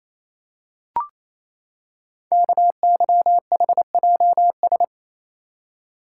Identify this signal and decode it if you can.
Morse code, heard as a keyed tone, about 27 wpm, 700 Hz. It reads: KYHJS
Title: Code Groups - 5 Letters - 28wpm